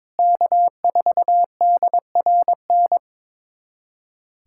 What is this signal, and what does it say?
Morse code, heard as a keyed tone, about 22 wpm, 700 Hz. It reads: K4DRN
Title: USA Callsign DE Callsign 22wpm